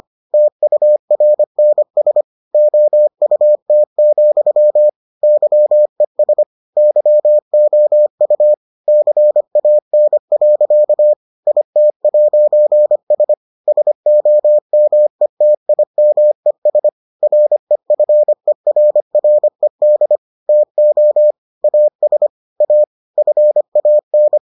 Morse code, heard as a keyed tone, about 25 wpm, 600 Hz. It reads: TURNS OUT, YES YOU CAN. IT'S SOMETIMES REFERRED TO AS A FAN